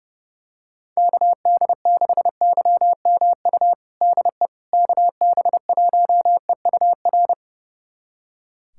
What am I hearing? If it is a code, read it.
Morse code, heard as a keyed tone, about 31 wpm, 700 Hz. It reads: KD6YMU DE KB1EUR
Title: USA Callsign DE Callsign - 6 Characters - 30wpm